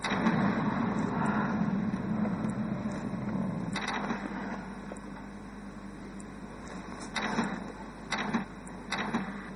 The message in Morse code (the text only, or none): none